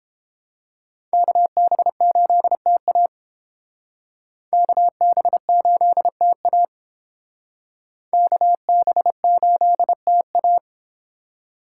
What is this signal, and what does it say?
Morse code, heard as a keyed tone, about 29 wpm, 700 Hz. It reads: KB8TA KB8TA KB8TA